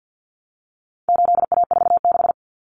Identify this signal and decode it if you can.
Morse code, heard as a keyed tone, about 49 wpm, 700 Hz. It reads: 8A46
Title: Common Cut Numbers - 4-Digit 50wpm